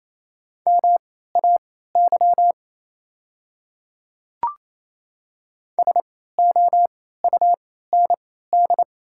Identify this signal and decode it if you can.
Morse code, heard as a keyed tone, about 28 wpm, spaced as for 16 wpm, 690 Hz. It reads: MAY SOUND
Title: 3x Character Spacing - Top 100 Words - 28wpm